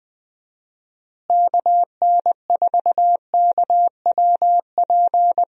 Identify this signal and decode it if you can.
Morse code, heard as a keyed tone, about 20 wpm, 700 Hz. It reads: KN4KWP
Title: USA Call Signs - 6 Characters - Repeated 2x - 20wpm